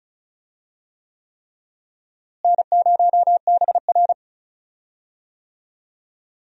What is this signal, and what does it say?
Morse code, heard as a keyed tone, about 36 wpm, 700 Hz. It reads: N0BR
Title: USA Call Signs - 4 Characters - Repeated 3x - 35wpm